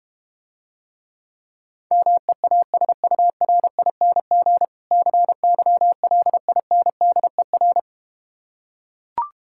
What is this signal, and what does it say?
Morse code, heard as a keyed tone, about 32 wpm, 700 Hz. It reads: MEASURING CYLINDER